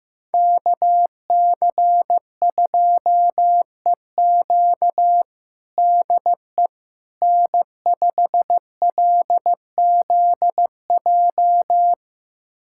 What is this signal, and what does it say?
Morse code, heard as a keyed tone, about 15 wpm, 700 Hz. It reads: KC2EQ DE N5LZJ
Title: USA Callsign DE Callsign - 5 Characters - 15wpm